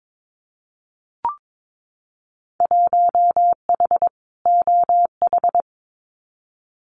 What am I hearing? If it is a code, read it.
Morse code, heard as a keyed tone, about 22 wpm, 700 Hz. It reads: 1H OH
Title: Winter Field Day Exchanges 22wpm